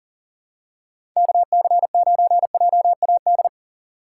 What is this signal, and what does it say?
Morse code, heard as a keyed tone, about 38 wpm, 700 Hz. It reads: KC9JAD
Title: USA Call Signs - 4x Character Spacing - 40wpm